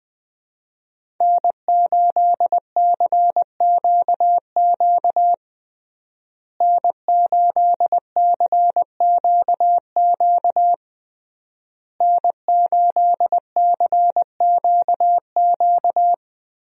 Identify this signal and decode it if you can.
Morse code, heard as a keyed tone, about 20 wpm, 700 Hz. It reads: N8CQQ N8CQQ N8CQQ
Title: USA Call Signs - 5 Characters - Repeated 3x - 20wpm